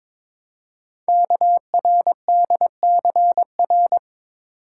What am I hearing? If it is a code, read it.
Morse code, heard as a keyed tone, about 22 wpm, 700 Hz. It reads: KRDCR
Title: Code Groups - 5 Letters - 22wpm